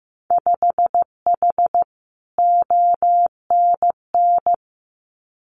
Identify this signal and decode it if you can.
Morse code, heard as a keyed tone, about 15 wpm, 700 Hz. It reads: 5H ONN
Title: Winter Field Day Exchanges Encore 15wpm